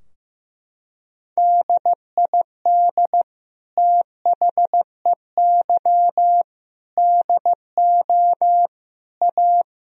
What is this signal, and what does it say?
Morse code, heard as a keyed tone, about 15 wpm, 700 Hz. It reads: DID THEY DO A
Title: Sentences from Top 1000 Words - Review Part 3 - 15wpm